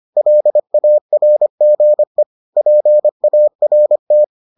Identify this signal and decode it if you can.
Morse code, heard as a keyed tone, about 25 wpm, 600 Hz. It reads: LARGE PART